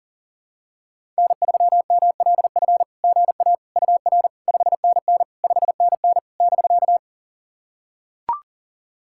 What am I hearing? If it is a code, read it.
Morse code, heard as a keyed tone, about 40 wpm, 700 Hz. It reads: N3MLF GA UR 5NN 5NN <BK>
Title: SOTA QSO Practice 40wpm